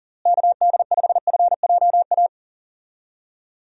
Morse code, heard as a keyed tone, about 40 wpm, 700 Hz. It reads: KD5FJA
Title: USA Call Signs - 6 Characters - Repeated 2x - 40wpm